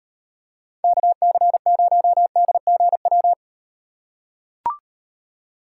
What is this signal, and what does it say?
Morse code, heard as a keyed tone, about 38 wpm, 700 Hz. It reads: KC0DGW